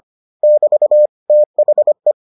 Morse code, heard as a keyed tone, about 25 wpm, 600 Hz. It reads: <BT> THE